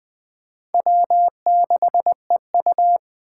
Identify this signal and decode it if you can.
Morse code, heard as a keyed tone, about 20 wpm, 700 Hz. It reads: W6EU
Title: K1USN SST QSO Practice - 20wpm